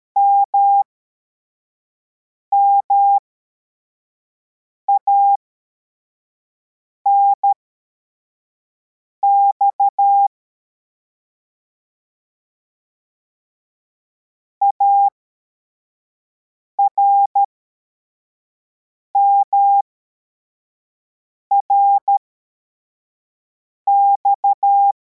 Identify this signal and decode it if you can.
Morse code, heard as a keyed tone, about 13 wpm, spaced as for 4 wpm, 800 Hz. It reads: MMANX ARMRX
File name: formation / cw / Text21